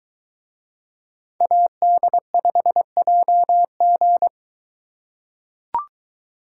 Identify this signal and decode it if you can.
Morse code, heard as a keyed tone, about 23 wpm, 700 Hz. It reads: AD5JG